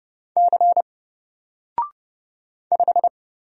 Morse code, heard as a keyed tone, about 30 wpm, 700 Hz. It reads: C 5
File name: Single-Letter-Number-30wpm